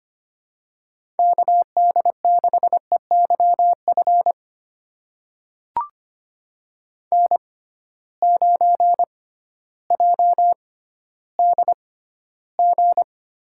Morse code, heard as a keyed tone, about 25 wpm, spaced as for 14 wpm, 700 Hz. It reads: KD6EYF N9JDG
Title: USA Call Signs - 6x Character Spacing - 25wpm